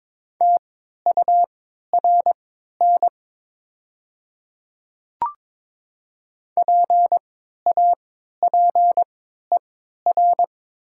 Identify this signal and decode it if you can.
Morse code, heard as a keyed tone, about 22 wpm, spaced as for 13 wpm, 700 Hz. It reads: TURN PAPER
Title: 3x Character Spacing - Top 300 Words - 22wpm